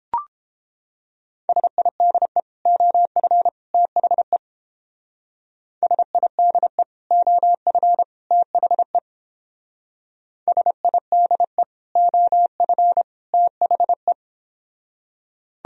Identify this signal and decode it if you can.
Morse code, heard as a keyed tone, about 29 wpm, 700 Hz. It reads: SIDE OF THE SIDE OF THE SIDE OF THE